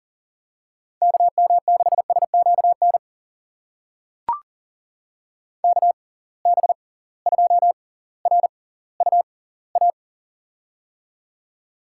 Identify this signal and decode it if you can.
Morse code, heard as a keyed tone, about 40 wpm, spaced as for 20 wpm, 700 Hz. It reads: KM6SQN KB2RUA
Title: USA Call Signs - 6 Characters - 6x Character Spacing - 40wpm